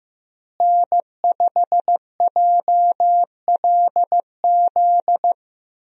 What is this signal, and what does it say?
Morse code, heard as a keyed tone, about 15 wpm, 700 Hz. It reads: N5JLZ
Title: USA Call Signs - 5 Characters - Repeated 2x - 15wpm